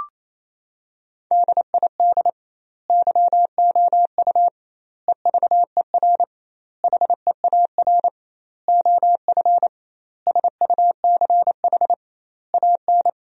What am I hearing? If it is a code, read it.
Morse code, heard as a keyed tone, about 28 wpm, 700 Hz. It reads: DID YOU EVER HEAR OF SUCH AN
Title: 2x Word-Spacing - Sentences from Top 400 Words - 28wpm